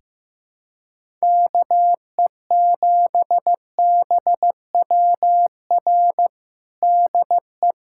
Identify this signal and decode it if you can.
Morse code, heard as a keyed tone, about 15 wpm, 700 Hz. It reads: KE7BWR DE
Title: USA Callsign DE Callsign - 6 Characters - 15wpm